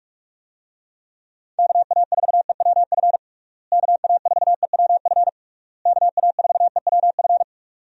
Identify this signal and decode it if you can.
Morse code, heard as a keyed tone, about 46 wpm, 700 Hz. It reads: KA4EWF KA4EWF KA4EWF
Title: USA Call Signs - 6 Characters - Repeated 3x - 45wpm